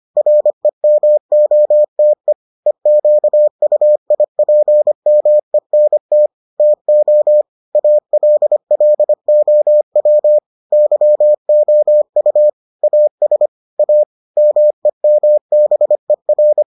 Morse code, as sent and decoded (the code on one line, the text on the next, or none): .-. -- --- - . / . --.- ..- .. .--. -- . -. - / - --- / .- .-.. .-.. --- .-- / -.-- --- ..- / .- ... / .- / -- . -- -... .-.
REMOTE EQUIPMENT TO ALLOW YOU AS A MEMBER